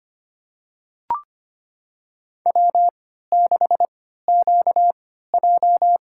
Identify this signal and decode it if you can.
Morse code, heard as a keyed tone, about 25 wpm, spaced as for 14 wpm, 700 Hz. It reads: W6QJ
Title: USA Call Signs - 4 Characters - 3x Character Spacing - 25wpm